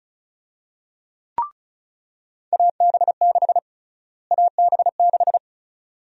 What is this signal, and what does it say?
Morse code, heard as a keyed tone, about 36 wpm, 700 Hz. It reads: AB6 AB6
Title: USA Call Sign Prefixes - Repeated 2x - 35wpm